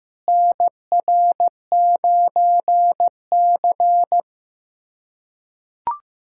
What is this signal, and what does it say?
Morse code, heard as a keyed tone, about 15 wpm, 700 Hz. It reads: NR9C
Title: USA Callsign DE Callsign - 4 Characters - 15wpm